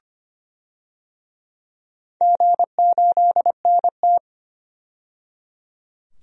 Morse code, heard as a keyed tone, about 25 wpm, 700 Hz. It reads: G8NT
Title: DXCC-100 Call Signs - 4 Characters - 25wpm